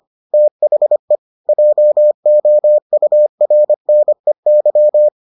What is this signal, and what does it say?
Morse code, heard as a keyed tone, about 25 wpm, 600 Hz. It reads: THE JOURNEY